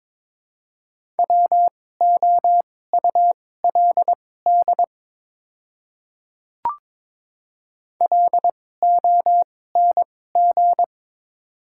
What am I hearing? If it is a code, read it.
Morse code, heard as a keyed tone, about 22 wpm, spaced as for 16 wpm, 700 Hz. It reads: WOULD LONG
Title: 2x Character Spacing - Top 100 Words - 22wpm